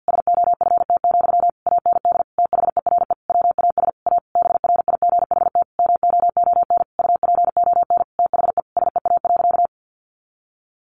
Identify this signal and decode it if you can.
Morse code, heard as a keyed tone, about 50 wpm, 700 Hz. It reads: SOFT, AND THERE WAS A BRIGHT MOON UPON THE SEA.
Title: Sentences from Top 1500 Words 50wpm